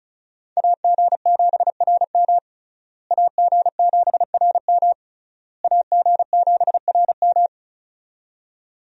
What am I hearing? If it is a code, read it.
Morse code, heard as a keyed tone, about 36 wpm, 700 Hz. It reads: AG7RM AG7RM AG7RM
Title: USA Call Signs - 5 Characters - Repeated 3x - 35wpm